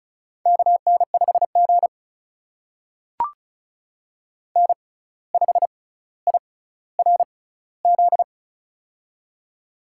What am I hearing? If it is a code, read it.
Morse code, heard as a keyed tone, about 36 wpm, spaced as for 17 wpm, 700 Hz. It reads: KN5G N5IRZ
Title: USA Call Signs - 6x Character Spacing - 35wpm